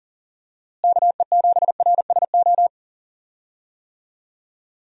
Morse code, heard as a keyed tone, about 37 wpm, 700 Hz. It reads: KE7RSO
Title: USA Call Signs - 6 Characters - Repeated 3x - 40wpm